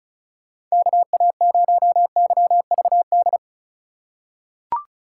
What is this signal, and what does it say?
Morse code, heard as a keyed tone, about 36 wpm, 700 Hz. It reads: KA0YVD
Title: USA Call Signs - 6 Characters - 2x Character Spacing - 35wpm